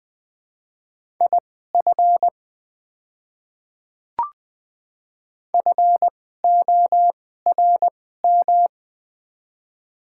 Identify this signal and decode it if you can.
Morse code, heard as a keyed tone, about 20 wpm, spaced as for 14 wpm, 700 Hz. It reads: IF FORM
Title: 2x Character Spacing - Top 200 Words - 20wpm